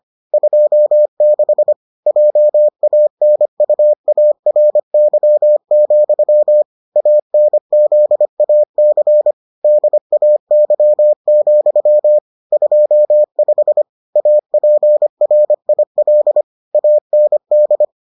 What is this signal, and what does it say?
Morse code, heard as a keyed tone, about 25 wpm, 600 Hz. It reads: 26 JANUARY, ANZAC DAY, 25 APRIL AND